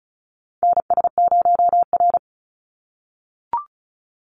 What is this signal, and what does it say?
Morse code, heard as a keyed tone, about 36 wpm, 700 Hz. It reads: NS0R
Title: Call Signs - 35wpm